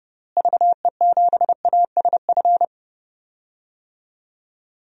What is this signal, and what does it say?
Morse code, heard as a keyed tone, about 32 wpm, 700 Hz. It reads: VE7ASF
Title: DXCC-100 Call Signs - 6 Characters - Repeated 3x - 30wpm